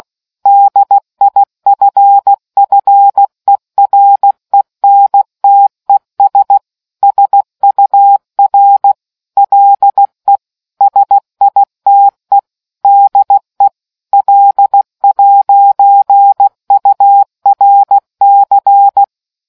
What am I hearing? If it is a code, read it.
Morse code, heard as a keyed tone, about 16 wpm, 800 Hz. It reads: DIFFERENTES SUR LE SITE DE L'URC